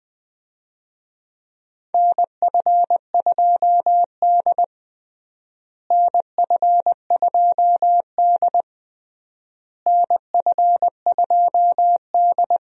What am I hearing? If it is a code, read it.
Morse code, heard as a keyed tone, about 20 wpm, 700 Hz. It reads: NF2D NF2D NF2D
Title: USA Call Signs - 4 Characters - Repeated 3x - 20wpm